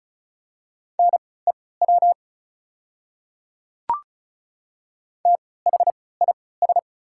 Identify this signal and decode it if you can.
Morse code, heard as a keyed tone, about 36 wpm, spaced as for 20 wpm, 700 Hz. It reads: NEW THIS
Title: 3x Character Spacing - Top 200 Words - 35wpm